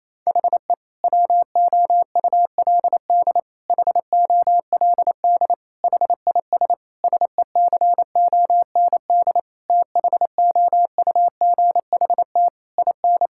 Morse code, heard as a keyed tone, about 28 wpm, 700 Hz. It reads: HE WOULD HOLD HIS SECOND THOUGHT IN
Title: Sentences from Top 1000 Words - Review Part 6 - 28wpm